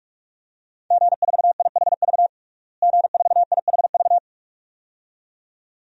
Morse code, heard as a keyed tone, about 42 wpm, 700 Hz. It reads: G4IHV G4IHV
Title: DXCC-100 Call Signs - 5 Characters - Repeated 2x - 45wpm